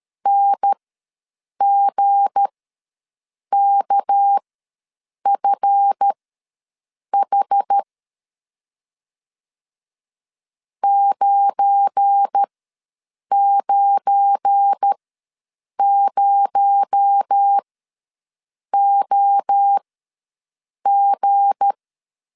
Morse code, heard as a keyed tone, about 13 wpm, spaced as for 7 wpm, 800 Hz. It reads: NGKFH 990OG
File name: LEZIONI / 7 GF9Z / 7 GF9 ws 12wpm 3344